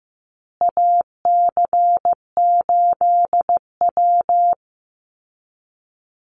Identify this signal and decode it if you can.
Morse code, heard as a keyed tone, about 15 wpm, 700 Hz. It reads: AC8W
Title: Call Signs Repeated 2x - 15wpm